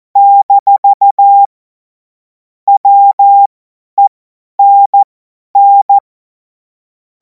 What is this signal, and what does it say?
Morse code, heard as a keyed tone, about 14 wpm, spaced as for 10 wpm, 800 Hz. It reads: - WENN
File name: Morsekurs / Witze / Witz 30. Juni 2026 50 F70